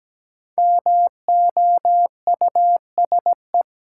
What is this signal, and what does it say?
Morse code, heard as a keyed tone, about 17 wpm, 700 Hz. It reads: MOUSE
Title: Sentences from Top 700 Words - New Word Drill - 17wpm